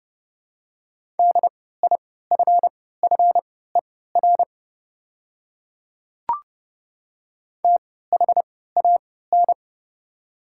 Morse code, heard as a keyed tone, about 30 wpm, spaced as for 17 wpm, 690 Hz. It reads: DIFFER THAN
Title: 3x Character Spacing - Top 200 Words - 30wpm